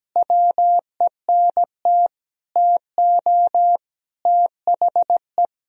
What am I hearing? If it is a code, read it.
Morse code, heard as a keyed tone, about 17 wpm, 700 Hz. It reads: WENT TO THE
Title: Sentences from Top 1000 Words - Review Part 2 - 17wpm